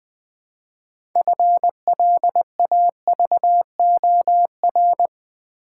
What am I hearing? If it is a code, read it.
Morse code, heard as a keyed tone, about 20 wpm, 700 Hz. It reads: FLAVOR
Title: Build A Word - 3 Letters At A Time - 6-Letter Words - 20wpm